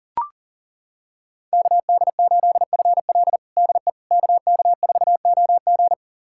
Morse code, heard as a keyed tone, about 40 wpm, 700 Hz. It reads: KD8FL DE KK4OG
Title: USA Callsign DE Callsign - 5 Characters - 40wpm